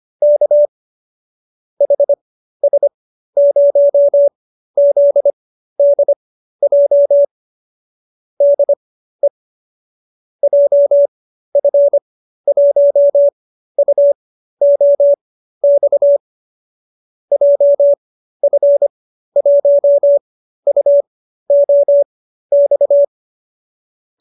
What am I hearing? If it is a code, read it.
Morse code, heard as a keyed tone, about 25 wpm, spaced as for 13 wpm, 590 Hz. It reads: K HS0ZDJ DE JF1UOX JF1UOX